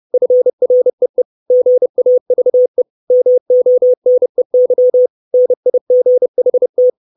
Morse code, heard as a keyed tone, about 30 wpm, 500 Hz. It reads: FREE GAVE MONEY NIGHT
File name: 632 Words 30WPM